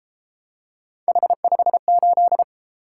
Morse code, heard as a keyed tone, about 32 wpm, 700 Hz. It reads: H58